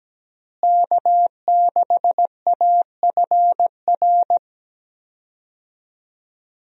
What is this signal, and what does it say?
Morse code, heard as a keyed tone, about 17 wpm, 700 Hz. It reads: K6AFR
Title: USA Call Signs - 5 Characters - Repeated 3x - 17wpm